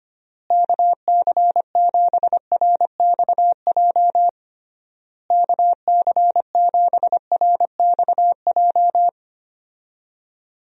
Morse code, heard as a keyed tone, about 25 wpm, 700 Hz. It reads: KC7RXJ KC7RXJ
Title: USA Call Signs - 6 Characters - Repeated 3x - 25wpm